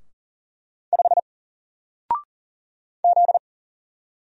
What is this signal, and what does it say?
Morse code, heard as a keyed tone, about 40 wpm, 700 Hz. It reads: 5 Z
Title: Single Letter-Number Alt 40wpm